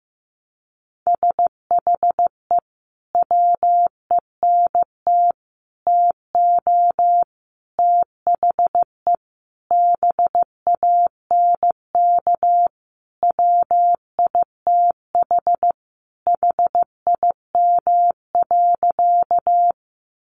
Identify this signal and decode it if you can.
Morse code, heard as a keyed tone, about 15 wpm, 700 Hz. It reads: SHE WENT TO THE BANK WITH HIM.
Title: Sentences from Top 900 Words 15wpm